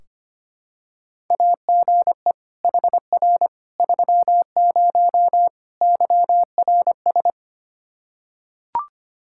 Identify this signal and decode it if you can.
Morse code, heard as a keyed tone, about 25 wpm, 700 Hz. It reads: AGE HR 30 YRS
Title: Practice QSO - Exchange 3 - More Info - 25wpm